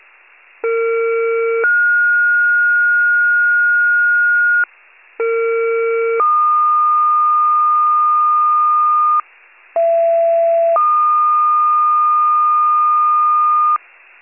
Two-tone page, heard as a one-second tone, then a three-second tone, about 460 then 1430 Hz, about 460 then 1220 Hz, about 670 then 1220 Hz. It is heard from 0.6 to 13.8 s.
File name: pager / 41 RESCUE DUTY / wcalert